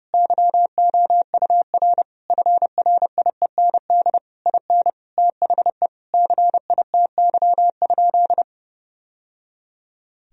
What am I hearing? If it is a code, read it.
Morse code, heard as a keyed tone, about 32 wpm, 700 Hz. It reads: YOUR FRIEND IN THE CITY?
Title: Sentences from Top 1000 Words - Review Part 1 - 30wpm